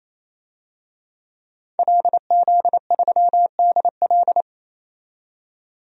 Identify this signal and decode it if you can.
Morse code, heard as a keyed tone, about 28 wpm, 700 Hz. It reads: LZ3DL
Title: DXCC-100 Call Signs - 5 Characters - 28wpm